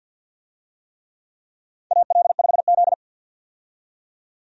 Morse code, heard as a keyed tone, about 51 wpm, 700 Hz. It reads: AL5Z
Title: USA Call Signs - 4 Characters - 50wpm